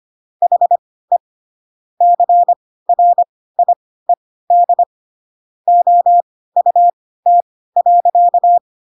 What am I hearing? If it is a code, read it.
Morse code, heard as a keyed tone, about 25 wpm, spaced as for 16 wpm, 700 Hz. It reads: HE CRIED OUT.
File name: Pinocchio / Pinocchio ch3 10 25 ik1hge